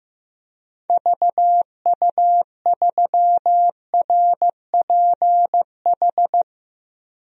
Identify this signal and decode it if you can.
Morse code, heard as a keyed tone, about 16 wpm, 700 Hz. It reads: VU3RPH
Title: DXCC-100 Call Signs - 6 Characters - 15wpm